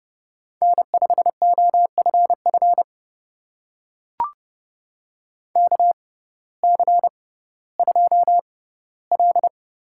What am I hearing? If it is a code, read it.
Morse code, heard as a keyed tone, about 29 wpm, spaced as for 16 wpm, 700 Hz. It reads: N5OFF KC2L
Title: USA Call Signs - 6x Character Spacing - 30wpm